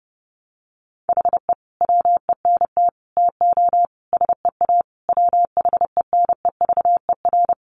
Morse code, heard as a keyed tone, about 30 wpm, 700 Hz. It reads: HE WENT TO SEA WHENEVER
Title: Sentences from Top 1200 Words 30wpm